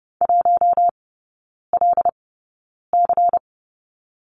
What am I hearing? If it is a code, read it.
Morse code, heard as a keyed tone, about 30 wpm, spaced as for 9 wpm, 700 Hz. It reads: 1LC
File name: Lesson 16e - Letter c - Three Characters 30wpm